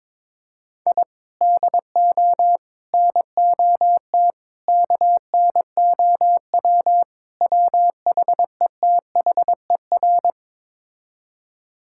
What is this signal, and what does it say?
Morse code, heard as a keyed tone, about 22 wpm, 700 Hz. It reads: I DO NOT KNOW WHETHER
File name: practice / 22wpm / Build A Phrase - 5-Word Phrases - 22wpm